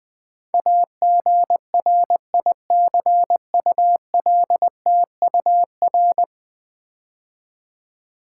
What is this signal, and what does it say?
Morse code, heard as a keyed tone, about 20 wpm, 700 Hz. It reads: AGRICULTUR
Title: Build A Word - 5 Letters At A Time - 12-Letter Words - 20wpm